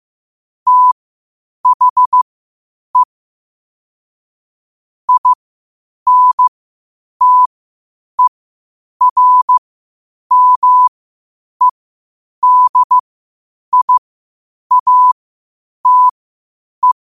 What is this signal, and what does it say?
Morse code, heard as a keyed tone, about 15 wpm, spaced as for 8 wpm, 1000 Hz. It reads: THE INTERMEDIATE
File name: qotdfiles / 8wpm